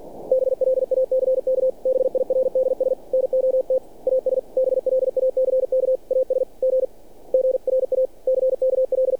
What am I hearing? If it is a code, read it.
Morse code, heard as a keyed tone, about 48 wpm, 520 Hz. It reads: BLACK BIRDS NOT AS BLACK AS G GRACKL
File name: archived / CRSnet